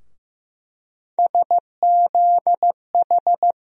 Morse code, heard as a keyed tone, about 15 wpm, 700 Hz. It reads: SZH
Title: USA Call Sign Postfixes - 3 Letters 15wpm